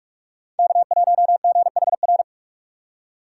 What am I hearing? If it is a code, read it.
Morse code, heard as a keyed tone, about 47 wpm, 700 Hz. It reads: K1GHR
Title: USA Call Signs - 5 Characters - 3x Character Spacing - 45wpm